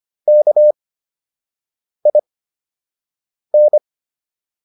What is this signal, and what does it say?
Morse code, heard as a keyed tone, about 25 wpm, spaced as for 6 wpm, 600 Hz. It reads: KIN